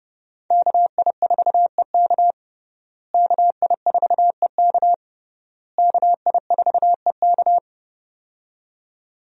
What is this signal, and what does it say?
Morse code, heard as a keyed tone, about 32 wpm, 700 Hz. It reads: KI4EK KI4EK KI4EK
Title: USA Call Signs - 5 Characters - Repeated 3x - 30wpm